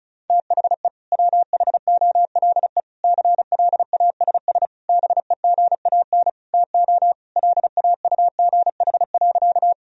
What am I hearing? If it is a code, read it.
Morse code, heard as a keyed tone, about 34 wpm, 700 Hz. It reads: THE WHOLE CLASS BEGAN TO LAUGH.
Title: Sentences from Top 1000 Words - Review Part 2 - 35wpm